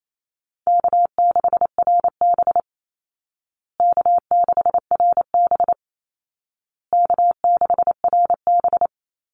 Morse code, heard as a keyed tone, about 28 wpm, 700 Hz. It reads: K6RB K6RB K6RB
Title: Call Signs Repeated 2x - 28wpm